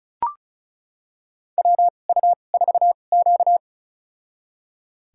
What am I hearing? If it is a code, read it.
Morse code, heard as a keyed tone, about 36 wpm, spaced as for 25 wpm, 700 Hz. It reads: WU4Q
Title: USA Call Signs - 4 Characters - 2x Character Spacing - 35wpm